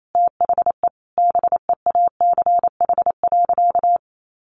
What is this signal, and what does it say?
Morse code, heard as a keyed tone, about 28 wpm, 700 Hz. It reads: THE BEACH.
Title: Sentences from Top 1500 Words 28wpm